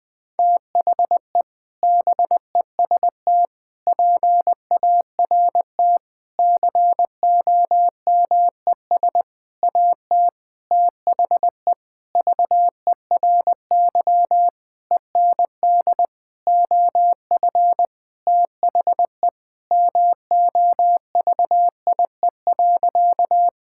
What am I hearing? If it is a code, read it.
Morse code, heard as a keyed tone, about 20 wpm, 700 Hz. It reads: THE BEST PART COMES AT THE VERY END OF THE MOVIE.
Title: Sentences with 6-Word Phrases - 20wpm